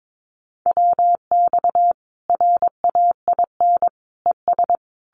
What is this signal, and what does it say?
Morse code, heard as a keyed tone, about 22 wpm, 700 Hz. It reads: WX RAIN ES
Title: Common QSO Phrases 22wpm